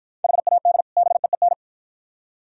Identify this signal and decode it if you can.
Morse code, heard as a keyed tone, about 49 wpm, 700 Hz. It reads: HAD BEEN